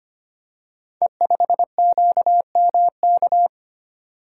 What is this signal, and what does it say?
Morse code, heard as a keyed tone, about 25 wpm, 700 Hz. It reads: E5QMK
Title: Code Groups - 5 Letter:Numbers - 25wpm